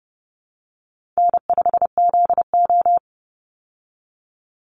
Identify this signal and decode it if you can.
Morse code, heard as a keyed tone, about 30 wpm, 700 Hz. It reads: N5ZO
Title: Easy Call Signs Repeated 2x - 30wpm